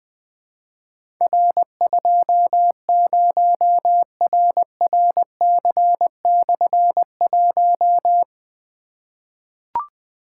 Morse code, heard as a keyed tone, about 20 wpm, 700 Hz. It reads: R20RRC/1
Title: Difficult Call Signs 20wpm